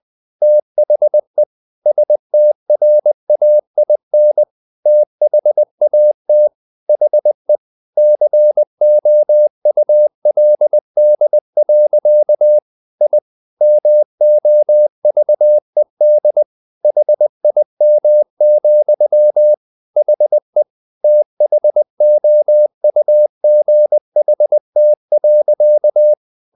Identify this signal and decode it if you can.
Morse code, heard as a keyed tone, about 20 wpm, 600 Hz. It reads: THE STRAIN THAT HE COULD. I MOVED HIM, HE THOUGHT.
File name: the old man and the sea part 2 20wpm